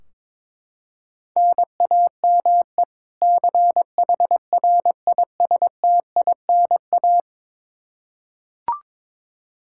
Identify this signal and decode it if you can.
Morse code, heard as a keyed tone, about 22 wpm, 700 Hz. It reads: NAME CHRISTINA
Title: Practice QSO - Exchange 1 - Beginning - 22wpm